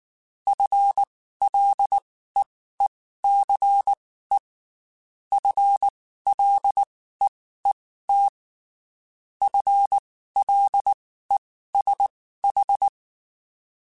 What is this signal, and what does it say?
Morse code, heard as a keyed tone, about 19 wpm, spaced as for 14 wpm, 800 Hz. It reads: FLEECE FLEET FLESH
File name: Spaced / FL Runs sp mp3s / FL Runs sp @18